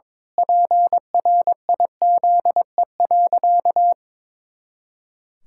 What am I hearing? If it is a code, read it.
Morse code, heard as a keyed tone, about 22 wpm, 700 Hz. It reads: PRIZE.
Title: Pangram Challenge v2 - 22wpm